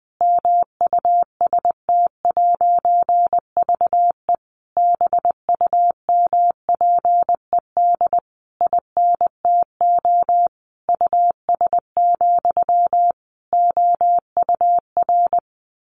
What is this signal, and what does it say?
Morse code, heard as a keyed tone, about 20 wpm, 700 Hz. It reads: MUST'VE BUMPED INTO US, OUR
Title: A Sound Of Thunder 20wpm